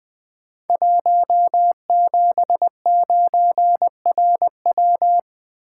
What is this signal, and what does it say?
Morse code, heard as a keyed tone, about 21 wpm, 700 Hz. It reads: 179RW
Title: Code Groups - 5 Letter:Numbers - 20wpm